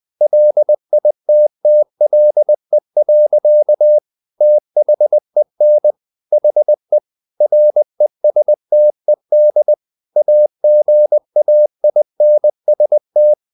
Morse code, heard as a keyed tone, about 20 wpm, 600 Hz. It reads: LITTLE. THEN HE RESTED AGAINST
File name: the old man and the sea / the old man and the sea part 1 20wpm